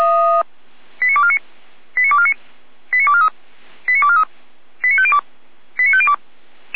Five-tone selective call (ZVEI), heard as quick runs of five tones, tones of about 70 ms. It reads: 89249 89249 89243 89243 89502 89502
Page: SDR/ZVEI